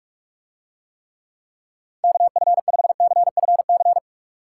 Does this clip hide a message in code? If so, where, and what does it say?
Yes: Morse code from 2.0 to 4.0 s, heard as a keyed tone, about 46 wpm, 700 Hz. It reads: KF5CFC